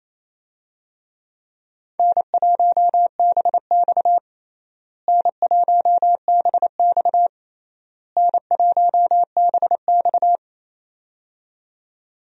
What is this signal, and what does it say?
Morse code, heard as a keyed tone, about 28 wpm, 700 Hz. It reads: N1BX N1BX N1BX
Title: USA Call Signs - 4 Characters - Repeated 3x - 28wpm